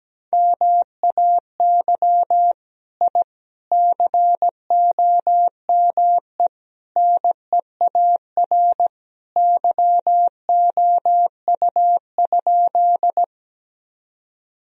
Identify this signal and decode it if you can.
Morse code, heard as a keyed tone, about 17 wpm, 700 Hz. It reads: MAY I COME NEAR YOU?
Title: Sentences from Top 1000 Words - Review Part 1 - 17wpm